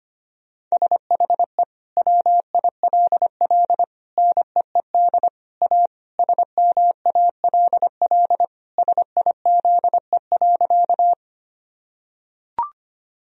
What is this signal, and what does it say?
Morse code, heard as a keyed tone, about 25 wpm, 700 Hz. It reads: SHE WILL NEED A SMALL SIZE.
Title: Sentences from Top 1000 Words - Review Part 2 - 25wpm